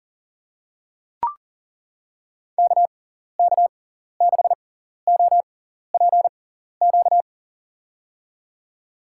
Morse code, heard as a keyed tone, about 40 wpm, spaced as for 14 wpm, 700 Hz. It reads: KK6OPQ
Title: USA Call Signs - 6x Character Spacing - 40wpm